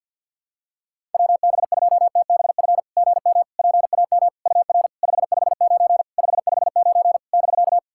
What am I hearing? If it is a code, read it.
Morse code, heard as a keyed tone, about 48 wpm, 690 Hz. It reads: WB2TBF GM PAM UR 559 559 <BK>
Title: SOTA QSO Practice 50wpm